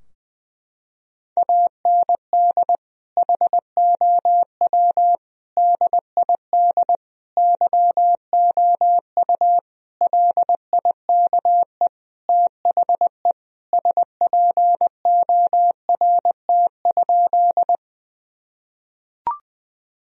Morse code, heard as a keyed tone, about 20 wpm, 700 Hz. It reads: AND HOW DID YOU LIKE THE SPORT?
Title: Sentences from Top 1000 Words - Review Part 4 - 20wpm